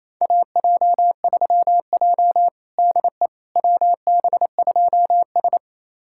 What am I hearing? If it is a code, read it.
Morse code, heard as a keyed tone, about 28 wpm, 700 Hz. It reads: AJ3J DE WB2S
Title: USA Callsign DE Callsign - 4 Characters - 28wpm